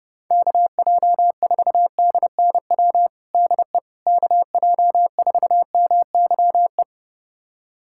Morse code, heard as a keyed tone, about 32 wpm, 700 Hz. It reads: KJ4DNW DE KJ4MYE
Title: USA Callsign DE Callsign - 6 Characters - 30wpm